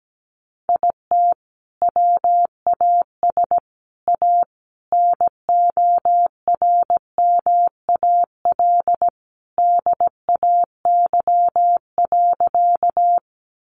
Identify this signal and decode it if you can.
Morse code, heard as a keyed tone, about 17 wpm, 700 Hz. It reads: IT WAS A NORMAL DAY.
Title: Sentences from Top 1200 Words 17wpm